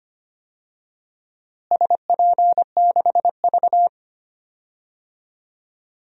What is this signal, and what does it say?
Morse code, heard as a keyed tone, about 25 wpm, 700 Hz. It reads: SP6V
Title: DXCC-100 Call Signs - 4 Characters - Repeated 3x - 25wpm